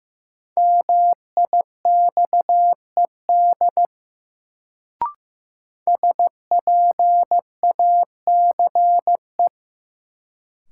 Morse code, heard as a keyed tone, about 15 wpm, 700 Hz. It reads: MIXED SPACE
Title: Sentences from Top 1500 Words - New Word Drill Alt - 15wpm